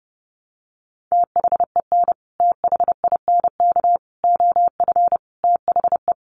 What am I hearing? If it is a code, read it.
Morse code, heard as a keyed tone, about 30 wpm, 700 Hz. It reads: THEN THINK OF THE
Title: Sentences from Top 1000 Words 30wpm